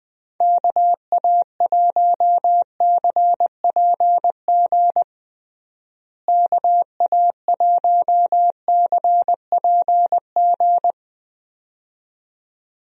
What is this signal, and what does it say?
Morse code, heard as a keyed tone, about 20 wpm, 700 Hz. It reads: KA1CPG KA1CPG
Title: USA Call Signs - 6 Characters - Repeated 2x - 20wpm